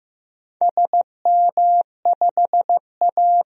Morse code, heard as a keyed tone, about 15 wpm, 700 Hz. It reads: SM5A
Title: DXCC-100 Call Signs - 4 Characters - 15wpm